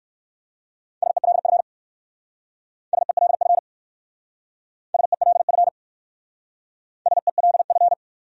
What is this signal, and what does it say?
Morse code, heard as a keyed tone, about 56 wpm, 700 Hz. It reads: SELF SELF SELF SELF
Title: Speed-Racing Top 200 Words 45wpm